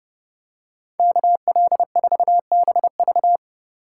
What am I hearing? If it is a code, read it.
Morse code, heard as a keyed tone, about 31 wpm, 700 Hz. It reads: KL4BV